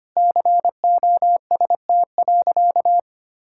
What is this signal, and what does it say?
Morse code, heard as a keyed tone, about 25 wpm, 700 Hz. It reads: COST.
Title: Sentences from Top 1000 Words - Review Part 5 - 25wpm